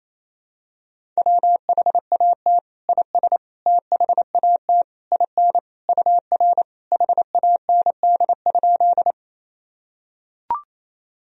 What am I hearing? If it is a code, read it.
Morse code, heard as a keyed tone, about 28 wpm, 700 Hz. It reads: WHAT IS THAT IN UR HAND?
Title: Sentences from Top 200 Words and Abbreviations 28wpm